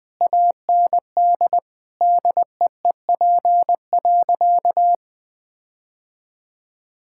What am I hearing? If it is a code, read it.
Morse code, heard as a keyed tone, about 20 wpm, 700 Hz. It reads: AND DEEP.
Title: Sentences from Top 1600 Words 20wpm